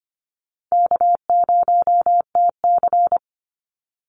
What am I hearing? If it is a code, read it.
Morse code, heard as a keyed tone, about 25 wpm, 700 Hz. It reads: K0TC
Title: Easy Call Signs - 25wpm